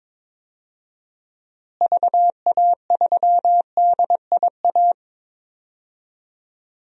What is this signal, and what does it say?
Morse code, heard as a keyed tone, about 22 wpm, 700 Hz. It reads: VA3DIA
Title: DXCC-100 Call Signs - 6 Characters - Repeated 2x - 22wpm